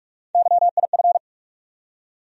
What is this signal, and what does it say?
Morse code, heard as a keyed tone, about 44 wpm, 700 Hz. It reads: YIF
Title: USA Call Sign Postfixes - 3 Letters 45wpm